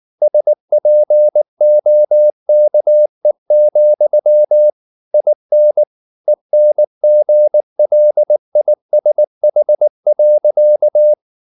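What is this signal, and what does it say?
Morse code, heard as a keyed tone, about 19 wpm, 600 Hz. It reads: SPOKE, IN ENGLISH.
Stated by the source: Morse code at 19 wpm from CW med Gnister, reading Ken Follett's novel Never in English.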